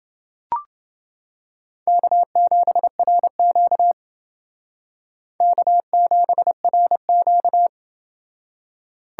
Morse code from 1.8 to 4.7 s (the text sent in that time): K7RQ